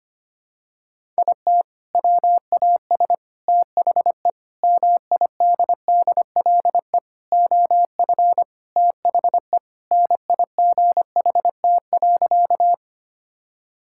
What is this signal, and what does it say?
Morse code, heard as a keyed tone, about 25 wpm, 700 Hz. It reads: IT WAS THE MIDDLE OF THE NIGHT.
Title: Sentences from Top 1000 Words - Review Part 6 - 25wpm